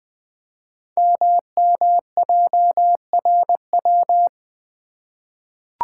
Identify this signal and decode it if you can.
Morse code, heard as a keyed tone, about 20 wpm, 700 Hz. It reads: MMJRW
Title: Code Groups - 5 Letters - 20wpm